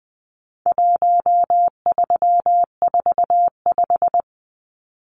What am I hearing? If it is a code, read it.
Morse code, heard as a keyed tone, about 20 wpm, 700 Hz. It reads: 1345
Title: Numbers - 4-Digit 20wpm